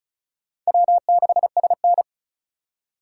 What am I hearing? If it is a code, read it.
Morse code, heard as a keyed tone, about 36 wpm, 700 Hz. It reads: W6SN
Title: USA Call Signs - 4 Characters - 2x Character Spacing - 35wpm